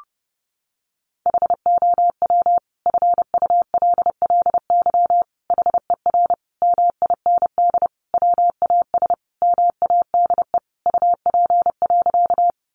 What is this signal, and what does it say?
Morse code, heard as a keyed tone, about 30 wpm, 700 Hz. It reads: HOW FULLY HER MIND WAS MADE UP.
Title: Sentences from Top 1300 Words 30wpm